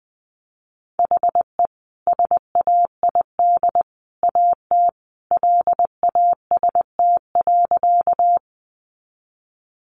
Morse code, heard as a keyed tone, about 20 wpm, 700 Hz. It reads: HE SAID AT LAST.
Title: Sentences from Top 300 Words 20wpm